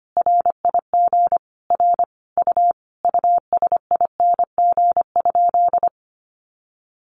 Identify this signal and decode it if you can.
Morse code, heard as a keyed tone, about 25 wpm, 700 Hz. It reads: RIG R U USING?
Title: Common QSO Phrases 25wpm